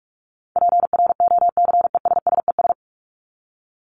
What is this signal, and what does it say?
Morse code, heard as a keyed tone, about 44 wpm, 700 Hz. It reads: PROCESSES
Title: ES:S Suffix - Top 100 Words with Base Words 45wpm